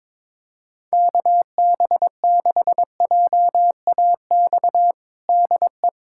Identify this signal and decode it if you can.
Morse code, heard as a keyed tone, about 22 wpm, 700 Hz. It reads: KB6JAX DE
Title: USA Callsign DE Callsign - 6 Characters - 22wpm